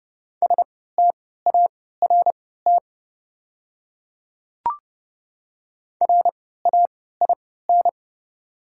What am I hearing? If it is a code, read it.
Morse code, heard as a keyed tone, about 30 wpm, spaced as for 17 wpm, 690 Hz. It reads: START RAIN
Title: 3x Character Spacing - Top 300 Words - 30wpm